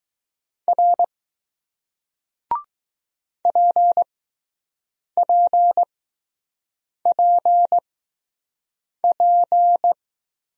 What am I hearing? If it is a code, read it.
Morse code, heard as a keyed tone, about 20 wpm, spaced as for 6 wpm, 700 Hz. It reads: R PPPP